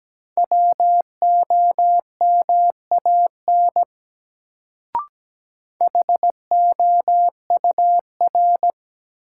Morse code, heard as a keyed tone, about 17 wpm, 700 Hz. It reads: WOMAN HOUR
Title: Sets of 5 Words - New Word Drill - 17wpm